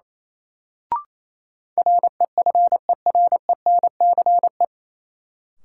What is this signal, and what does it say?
Morse code, heard as a keyed tone, about 28 wpm, 700 Hz. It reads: REFERENCE
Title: Sentences from Top 1200 Words - New Word Drill Alt - 28wpm